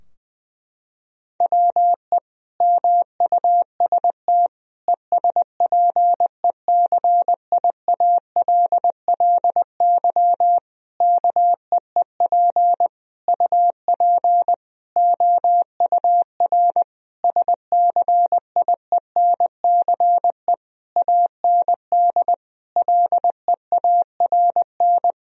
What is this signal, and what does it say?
Morse code, heard as a keyed tone, about 20 wpm, 700 Hz. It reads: WE MUST ESPECIALLY KEEP UP OUR SCIENCE AND LEARN
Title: Sentences from Top 1000 Words - Review Part 4 - 20wpm